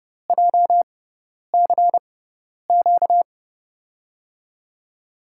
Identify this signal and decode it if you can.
Morse code, heard as a keyed tone, about 29 wpm, spaced as for 10 wpm, 700 Hz. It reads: JCQ
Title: USA Call Signs - 6 Characters - 6x Character Spacing - 30wpm